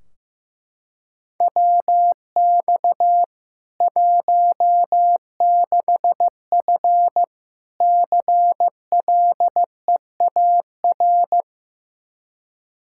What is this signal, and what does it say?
Morse code, heard as a keyed tone, about 16 wpm, 700 Hz. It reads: WX 16F CLEAR
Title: K1USN SST QSO Practice 15wpm